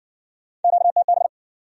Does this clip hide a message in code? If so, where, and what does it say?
Yes: Morse code from 0.6 to 1.3 s, heard as a keyed tone, about 59 wpm, 700 Hz. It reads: QTB